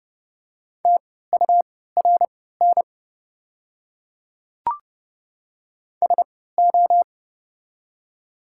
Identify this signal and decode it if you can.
Morse code, heard as a keyed tone, about 30 wpm, spaced as for 17 wpm, 700 Hz. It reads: TURN SO
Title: 3x Character Spacing - Top 200 Words - 30wpm